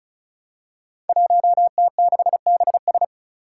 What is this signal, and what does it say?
Morse code, heard as a keyed tone, about 36 wpm, 700 Hz. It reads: 1T6BS
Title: Five Challenging Letter-Number Combinations 35wpm